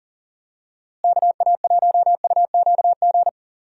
Morse code, heard as a keyed tone, about 40 wpm, 700 Hz. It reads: KA1UQG
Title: USA Call Signs - 6 Characters - Repeated 3x - 40wpm